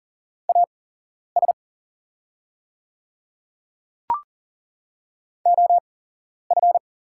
Morse code, heard as a keyed tone, about 40 wpm, spaced as for 11 wpm, 690 Hz. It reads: AS OF
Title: 8x Character Spacing - Top 300 Words - 40wpm